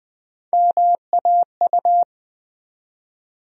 Morse code, heard as a keyed tone, about 21 wpm, 700 Hz. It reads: MAU
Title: USA Call Sign Postfixes - 3 Letters 20wpm